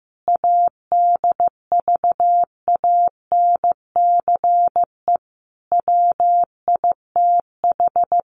Morse code, heard as a keyed tone, about 14 wpm, 700 Hz. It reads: ADVANCE WITH
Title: Sentences from Top 1400 Words 15wpm